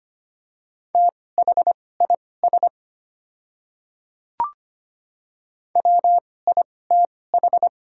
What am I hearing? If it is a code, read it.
Morse code, heard as a keyed tone, about 25 wpm, spaced as for 18 wpm, 700 Hz. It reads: THIS WITH
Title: 2x Character Spacing - Top 100 Words - 25wpm